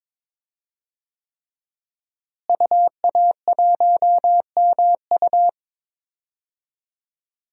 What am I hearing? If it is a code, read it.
Morse code, heard as a keyed tone, about 22 wpm, 700 Hz. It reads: UA1MU
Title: DXCC-100 Call Signs - 5 Characters - Repeated 3x - 22wpm